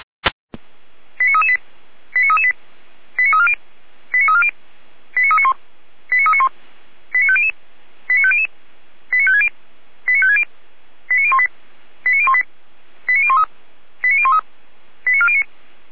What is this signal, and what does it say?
Five-tone selective call (ZVEI), heard as quick runs of five tones, tones of about 70 ms. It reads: 89338 89338 89340 89340 89391 89391 89500 89500 89560 89560 89018 89018 89013 89013 89409